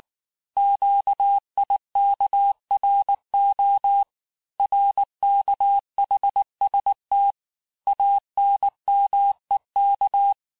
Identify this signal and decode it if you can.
Morse code, heard as a keyed tone, about 19 wpm, 800 Hz. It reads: QIKRO RKHST ANMEK
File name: cours-du-lundi files / cours-1589-texte-chiffre-vitesse-18mpm